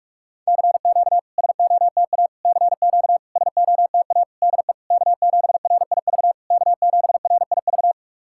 Morse code, heard as a keyed tone, about 44 wpm, 690 Hz. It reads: CQ SOTA CQ SOTA DE K7RIV K7RIV
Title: SOTA QSO Practice 45wpm